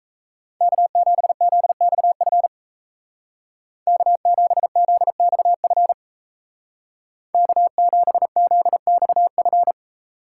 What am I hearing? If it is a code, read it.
Morse code, heard as a keyed tone, about 35 wpm, 700 Hz. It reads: K7ZXF K7ZXF K7ZXF